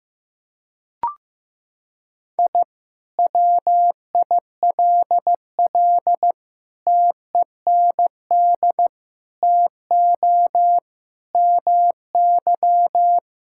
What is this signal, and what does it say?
Morse code, heard as a keyed tone, about 16 wpm, 700 Hz. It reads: I WILL TEND TO MY
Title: Sentences from Top 1000 Words - Review Part 5 - 15wpm